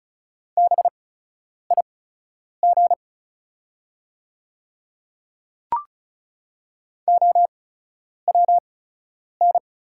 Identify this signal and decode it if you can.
Morse code, heard as a keyed tone, about 36 wpm, spaced as for 10 wpm, 700 Hz. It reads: BIG OWN
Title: 8x Character Spacing - Top 300 Words - 35wpm